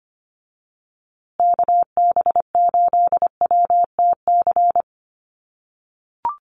Morse code, heard as a keyed tone, about 25 wpm, 700 Hz. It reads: KB8WTC
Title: USA Call Signs Repeated 2x 25wpm